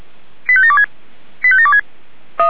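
Five-tone selective call (ZVEI), heard as quick runs of five tones, tones of about 70 ms. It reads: 85726 85726
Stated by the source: ZVEI five-tone call received on SDR.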